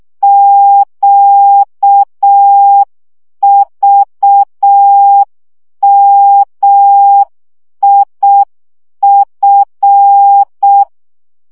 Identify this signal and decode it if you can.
Morse code, heard as a keyed tone, about 6 wpm, 800 Hz. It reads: QVMIF
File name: morze vizsga 2011-3 5